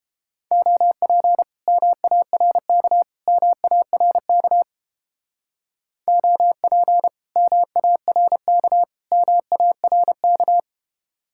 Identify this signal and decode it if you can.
Morse code, heard as a keyed tone, about 30 wpm, 700 Hz. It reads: OP MARK MARK OP MARK MARK